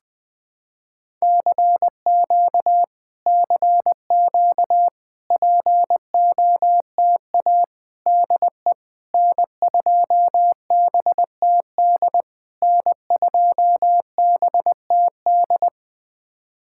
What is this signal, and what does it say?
Morse code, heard as a keyed tone, about 20 wpm, 700 Hz. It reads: CQ CQ POTA DE N2BTD N2BTD
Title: POTA QSO Practice - 20wpm